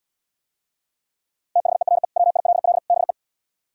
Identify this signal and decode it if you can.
Morse code, heard as a keyed tone, about 76 wpm, 700 Hz. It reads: THERE WILL BE